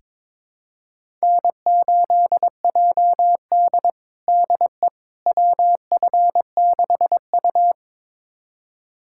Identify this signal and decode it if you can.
Morse code, heard as a keyed tone, about 22 wpm, 700 Hz. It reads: N8JD DE WF6U
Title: USA Callsign DE Callsign - 4 Characters - 22wpm